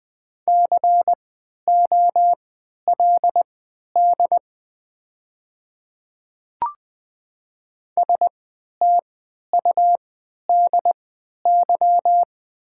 Morse code, heard as a keyed tone, about 20 wpm, spaced as for 11 wpm, 700 Hz. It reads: COLD STUDY
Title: 3x Character Spacing - Top 500 Words - 20wpm